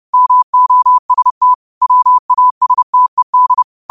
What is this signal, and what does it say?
Morse code, heard as a keyed tone, about 30 wpm, 1000 Hz. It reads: MOST WASTED
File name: 30wpm